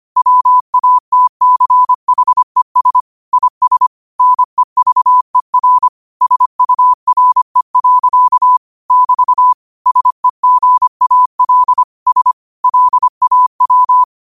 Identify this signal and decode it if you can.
Morse code, heard as a keyed tone, about 25 wpm, 1000 Hz. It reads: WATCHES IS NEVER SURE. <BT> SEGAL S LAW
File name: qotdfiles / 25wpm